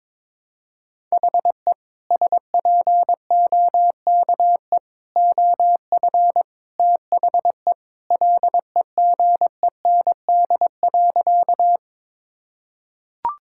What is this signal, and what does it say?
Morse code, heard as a keyed tone, about 22 wpm, 700 Hz. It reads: HE SPOKE OF THE LEGEND.
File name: practice / 22wpm / Sentences with 2-Word Phrases - 22wpm